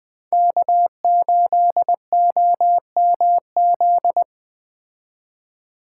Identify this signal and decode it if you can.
Morse code, heard as a keyed tone, about 20 wpm, 700 Hz. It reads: K8OMZ